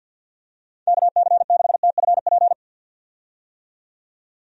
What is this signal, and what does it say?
Morse code, heard as a keyed tone, about 50 wpm, 700 Hz. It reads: KC6TFP
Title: USA Call Signs - 6 Characters - 50wpm